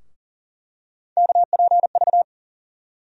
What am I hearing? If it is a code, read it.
Morse code, heard as a keyed tone, about 40 wpm, 700 Hz. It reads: KPV
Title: USA Call Sign Postfixes - 3 Letters 40wpm